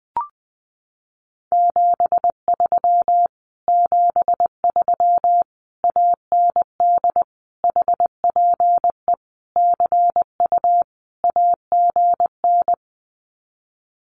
Morse code, heard as a keyed tone, about 20 wpm, 700 Hz. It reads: 73 73 AND HPE CU AGN
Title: Common QSO Phrases Encore 20wpm